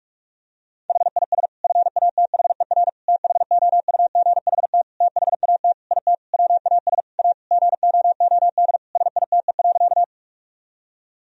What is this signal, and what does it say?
Morse code, heard as a keyed tone, about 41 wpm, 700 Hz. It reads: HIS FATHER THOUGHT THAT IT WAS A GOOD SITE.
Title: Sentences from Top 1000 Words - Review Part 4 - 45wpm